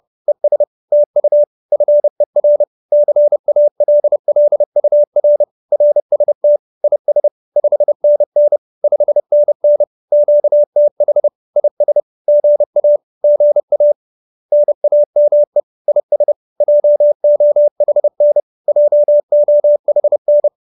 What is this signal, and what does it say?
Morse code, heard as a keyed tone, about 30 wpm, 590 Hz. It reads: ES TU FER CALLUR RST IS 5NN 5NN QTH IS GA GA NAME IS JOHN JOHN